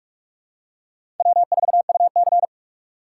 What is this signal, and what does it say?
Morse code, heard as a keyed tone, about 46 wpm, 700 Hz. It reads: W4UC
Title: USA Call Signs - 4 Characters - 2x Character Spacing - 45wpm